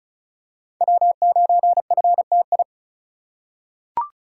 Morse code, heard as a keyed tone, about 36 wpm, 700 Hz. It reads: W9FTI
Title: USA Call Signs - 3x Character Spacing - 35wpm